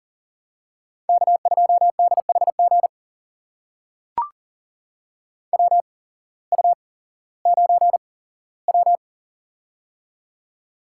Morse code, heard as a keyed tone, about 40 wpm, spaced as for 19 wpm, 700 Hz. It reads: K2DHG WU9W
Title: USA Call Signs - 8x Character Spacing - 40wpm